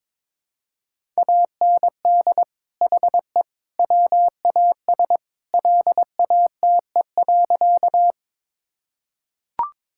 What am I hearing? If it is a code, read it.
Morse code, heard as a keyed tone, about 21 wpm, 700 Hz. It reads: AND HE WAS LATE.
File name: practice / 22wpm / Sentences with 3-Word Phrases - 22wpm